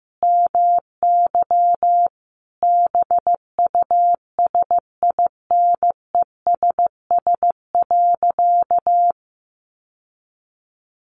Morse code, heard as a keyed tone, about 15 wpm, 700 Hz. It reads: MY BUSINESS.
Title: Sentences from Top 600 Words 15wpm